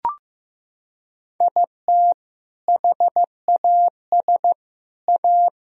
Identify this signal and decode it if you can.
Morse code, heard as a keyed tone, about 15 wpm, 700 Hz. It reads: IT HAS A
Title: Sentences from Top 1000 Words - Review Part 1 - 15wpm